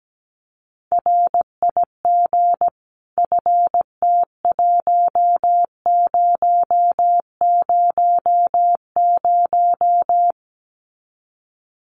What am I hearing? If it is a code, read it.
Morse code, heard as a keyed tone, about 17 wpm, 700 Hz. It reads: RIG FT1000
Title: Common QSO Phrases 17wpm